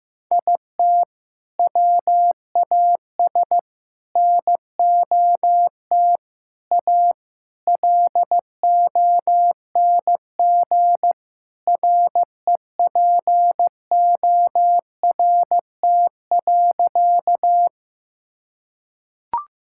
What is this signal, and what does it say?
Morse code, heard as a keyed tone, about 15 wpm, 700 Hz. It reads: IT WAS NOT A LONG REPORT.
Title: Sentences from Top 1000 Words - Review Part 6 - 15wpm